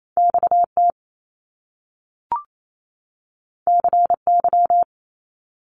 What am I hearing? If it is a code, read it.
Morse code, heard as a keyed tone, about 28 wpm, 700 Hz. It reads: XT CY
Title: Top Two Letter Combinations in Words 28wpm